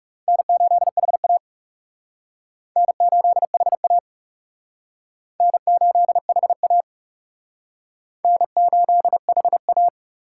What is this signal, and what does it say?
Morse code, heard as a keyed tone, about 36 wpm, 700 Hz. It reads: N8HA N8HA N8HA N8HA
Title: Speed Racing - USA 1x2 Call Signs 30wpm